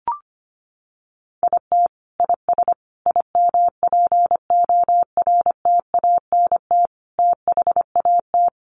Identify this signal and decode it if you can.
Morse code, heard as a keyed tone, about 25 wpm, 700 Hz. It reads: IT IS IMPORTANT THAT
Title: Sentences from Top 600 Words 25wpm